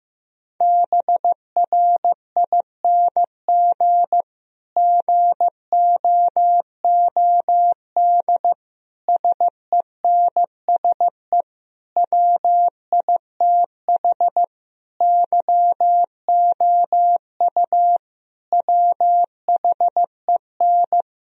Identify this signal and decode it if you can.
Morse code, heard as a keyed tone, about 15 wpm, 700 Hz. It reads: BRING GOOD SENSE WITH YOU WHEN
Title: Sentences from Top 1000 Words - Review Part 4 - 15wpm